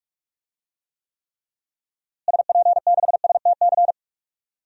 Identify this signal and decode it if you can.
Morse code, heard as a keyed tone, about 44 wpm, 700 Hz. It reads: SP6STC
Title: DXCC-100 Call Signs - 6 Characters - Repeated 3x - 45wpm